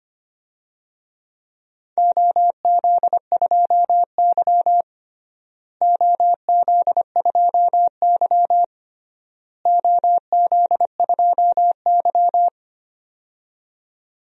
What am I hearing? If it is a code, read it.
Morse code, heard as a keyed tone, about 25 wpm, 700 Hz. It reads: OZ2Y OZ2Y OZ2Y
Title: DXCC-100 Call Signs - 4 Characters - Repeated 3x - 25wpm